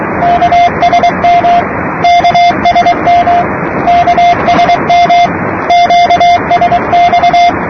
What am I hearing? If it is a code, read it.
Morse code, heard as a keyed tone, about 23 wpm, 700 Hz, with noise. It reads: KSM KSM KSM QSX
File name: namerica / usa / usa-8280-ksm-22445-8-khz-15-03-2008-2125-utc-gschwartz